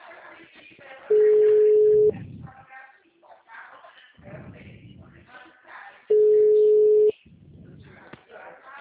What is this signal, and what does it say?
Ringback tone (Europe, 425 Hz), with speech.
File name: out- -2022- - - .4547